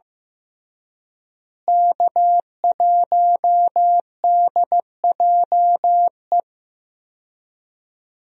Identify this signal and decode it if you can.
Morse code, heard as a keyed tone, about 15 wpm, 700 Hz. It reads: K1DJE
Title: USA Call Signs - 5 Characters - Repeated 3x - 15wpm